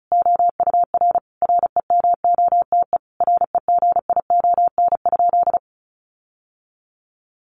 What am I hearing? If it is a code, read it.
Morse code, heard as a keyed tone, about 34 wpm, 700 Hz. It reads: OUR REMOTE REGION?
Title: Sentences from Top 1500 Words 35wpm